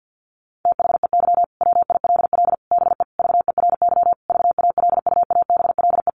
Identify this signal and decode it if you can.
Morse code, heard as a keyed tone, about 50 wpm, 700 Hz. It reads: THEY WILL BE VERY VALUABLE
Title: Sentences from Top 1100 Words 50wpm